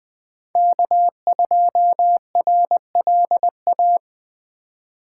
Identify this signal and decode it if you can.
Morse code, heard as a keyed tone, about 20 wpm, 700 Hz. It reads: K2RLA
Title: Code Groups - 5 Letter:Numbers - 20wpm